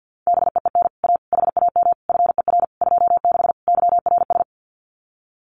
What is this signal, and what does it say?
Morse code, heard as a keyed tone, about 50 wpm, 700 Hz. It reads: BEEN A HAM FER 26 YRS
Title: Common QSO Phrases Encore 50wpm